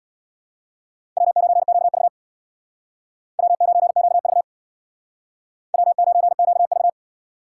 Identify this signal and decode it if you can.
Morse code, heard as a keyed tone, about 63 wpm, 700 Hz. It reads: W9QV W9QV W9QV